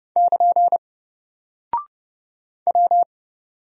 Morse code, heard as a keyed tone, about 30 wpm, 700 Hz. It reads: <KN> W
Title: Single Character (Letter:Number:Punctuation:Prosign) Alt 30wpm